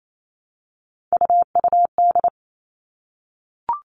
Morse code, heard as a keyed tone, about 28 wpm, 700 Hz. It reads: UUD
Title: All Cut Numbers - 3-Digit 28wpm